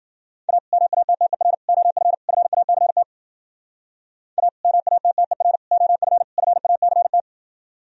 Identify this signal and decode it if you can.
Morse code, heard as a keyed tone, about 49 wpm, 700 Hz. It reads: A MATTER OF FACT A MATTER OF FACT
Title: Speed-Racing - Sets of 4 Words 40wpm